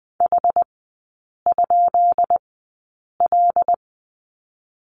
Morse code, heard as a keyed tone, about 20 wpm, spaced as for 8 wpm, 700 Hz. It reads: H?L
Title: 152 Lesson 25e - Letter p - Three Characters 20wpm